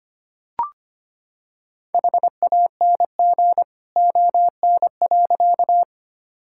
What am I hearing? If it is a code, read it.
Morse code, heard as a keyed tone, about 25 wpm, 700 Hz. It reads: HANG ON.
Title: Sentences from Top 1600 Words 25wpm